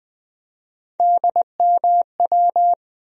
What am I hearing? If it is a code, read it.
Morse code, heard as a keyed tone, about 20 wpm, 700 Hz. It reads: DMW
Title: USA Call Sign Postfixes - 3 Letters 20wpm